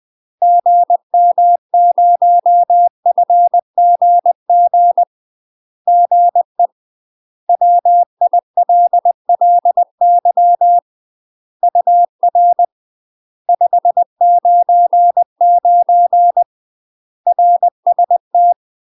Morse code, heard as a keyed tone, about 20 wpm, 700 Hz. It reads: GM0FGG GE WILLY UR 599 RST
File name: UK MASTER QSOs TRAINING / 20WPM / UK MASTER QSOs 0012